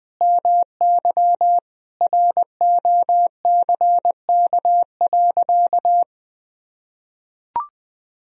Morse code, heard as a keyed tone, about 21 wpm, 700 Hz. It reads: MY ROCK.
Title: Sentences from Top 1000 Words - Review Part 2 - 20wpm